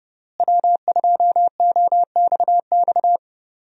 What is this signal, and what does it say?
Morse code, heard as a keyed tone, about 32 wpm, 700 Hz. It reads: W2OXX
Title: USA Call Signs - 5 Characters - Repeated 2x - 30wpm